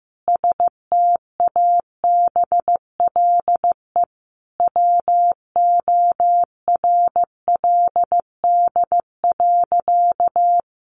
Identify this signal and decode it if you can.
Morse code, heard as a keyed tone, about 15 wpm, 700 Hz. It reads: STABLE WORLD.
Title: Sentences from Top 1500 Words 15wpm